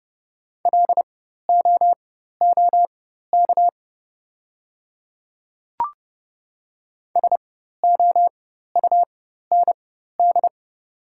Morse code, heard as a keyed tone, about 30 wpm, spaced as for 14 wpm, 700 Hz. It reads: LOOK SOUND
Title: 4x Character Spacing - Top 100 Words - 30wpm